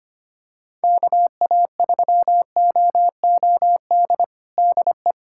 Morse code, heard as a keyed tone, about 25 wpm, 700 Hz. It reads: KA3OOD DE
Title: USA Callsign DE Callsign 25wpm